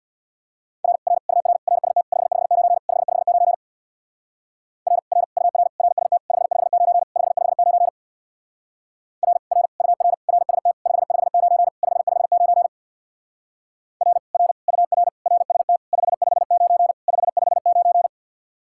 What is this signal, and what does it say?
Morse code, heard as a keyed tone, about 61 wpm, 700 Hz. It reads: R R UR RST 559 559 R R UR RST 559 559 R R UR RST 559 559 R R UR RST 559 559
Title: Speed-Racing Common QSO Phrases - 50wpm